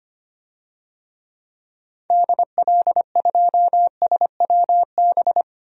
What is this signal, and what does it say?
Morse code, heard as a keyed tone, about 25 wpm, 700 Hz. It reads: DL2SWB
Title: DXCC-100 Call Signs - 6 Characters - 25wpm